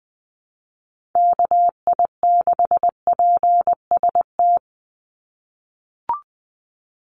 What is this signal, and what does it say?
Morse code, heard as a keyed tone, about 20 wpm, 700 Hz. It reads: KI6PST
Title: USA Call Signs 20wpm